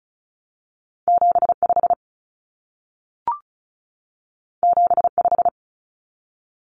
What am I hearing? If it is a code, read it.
Morse code, heard as a keyed tone, about 36 wpm, 700 Hz. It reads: 75 75
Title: Common Cut Numbers - 2-Digit 35wpm